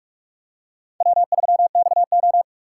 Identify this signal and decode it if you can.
Morse code, heard as a keyed tone, about 46 wpm, 700 Hz. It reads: W3XO